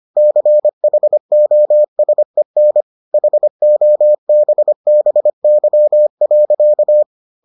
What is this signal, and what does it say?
Morse code, heard as a keyed tone, about 25 wpm, 600 Hz. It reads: CHOSEN HOBBY.